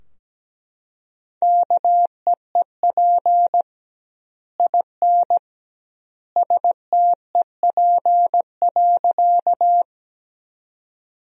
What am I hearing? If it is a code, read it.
Morse code, heard as a keyed tone, about 17 wpm, 700 Hz. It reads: KEEP IN STEP.
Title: 2x Word-Spacing - Sentences from Top 400 Words - 17wpm